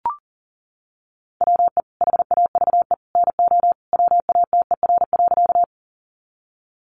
Morse code, heard as a keyed tone, about 40 wpm, 700 Hz. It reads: WE HAVE NO WATER.
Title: Sentences from Top 100 Words 40wpm